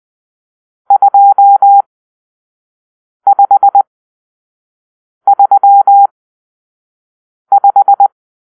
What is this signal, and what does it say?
Morse code, heard as a keyed tone, about 20 wpm, spaced as for 5 wpm, 800 Hz. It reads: 2535
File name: arrl / Grp-6 Drill-4